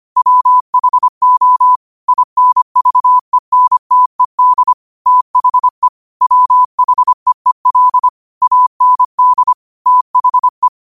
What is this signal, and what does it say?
Morse code, heard as a keyed tone, about 25 wpm, 1000 Hz. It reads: WHO INVENTED THE WHEEL AND THE